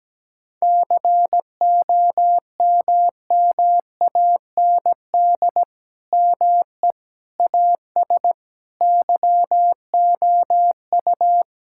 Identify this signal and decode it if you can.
Morse code, heard as a keyed tone, about 17 wpm, 700 Hz. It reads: COMMAND ME AS YOU
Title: Sentences from Top 1600 Words 17wpm